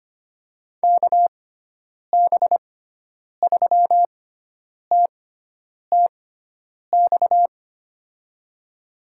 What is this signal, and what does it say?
Morse code, heard as a keyed tone, about 25 wpm, spaced as for 9 wpm, 700 Hz. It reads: KB3TTX
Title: USA Call Signs - 6 Characters - 6x Character Spacing - 25wpm